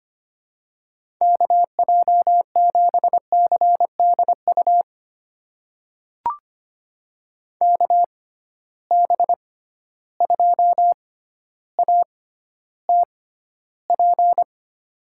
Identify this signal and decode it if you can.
Morse code, heard as a keyed tone, about 25 wpm, spaced as for 13 wpm, 700 Hz. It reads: KJ7CDU KB2ATP
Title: USA Call Signs - 6 Characters - 6x Character Spacing - 25wpm